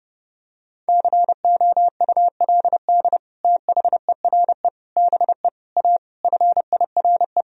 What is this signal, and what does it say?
Morse code, heard as a keyed tone, about 30 wpm, 700 Hz. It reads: COULD THERE BE A FIRE
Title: Sentences from Top 1000 Words - Review Part 4 - 30wpm